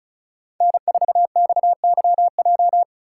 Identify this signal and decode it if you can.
Morse code, heard as a keyed tone, about 36 wpm, 700 Hz. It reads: N4XYJ
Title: USA Call Signs - 5 Characters - 2x Character Spacing - 35wpm